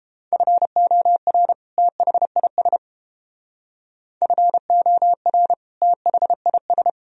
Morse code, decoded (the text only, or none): FOR THIS FOR THIS